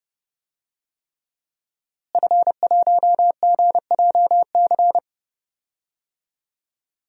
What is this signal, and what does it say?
Morse code, heard as a keyed tone, about 30 wpm, 700 Hz. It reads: F1GJC
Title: DXCC-100 Call Signs - 5 Characters - Repeated 3x - 30wpm